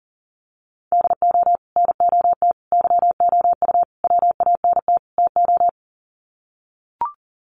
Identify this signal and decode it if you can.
Morse code, heard as a keyed tone, about 40 wpm, 700 Hz. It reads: DO NOT YOU WANT TO
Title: Sets of 5 Words Encore2 40wpm